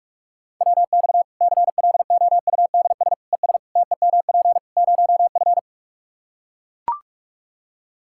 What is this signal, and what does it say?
Morse code, heard as a keyed tone, about 41 wpm, 700 Hz. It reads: WX CLOUDS ES TEMP 0F
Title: Practice QSO - Exchange 2 - Info - 45wpm